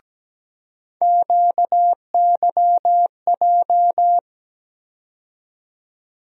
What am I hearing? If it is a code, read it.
Morse code, heard as a keyed tone, about 17 wpm, 700 Hz. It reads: QYJ
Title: USA Call Sign Postfixes - 3 Letters 17wpm